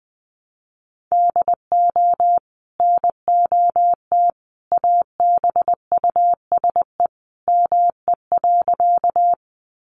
Morse code, heard as a keyed tone, about 20 wpm, 700 Hz. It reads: DO NOT ABUSE ME.
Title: Sentences from Top 1300 Words 20wpm